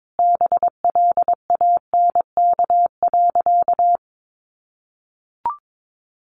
Morse code, heard as a keyed tone, about 22 wpm, 700 Hz. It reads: BLANK.
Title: Sentences from Top 1500 Words 22wpm